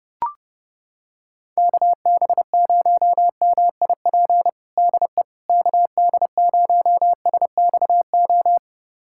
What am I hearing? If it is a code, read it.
Morse code, heard as a keyed tone, about 30 wpm, 700 Hz. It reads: KB0MIP DE KD0SXO
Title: USA Callsign DE Callsign 30wpm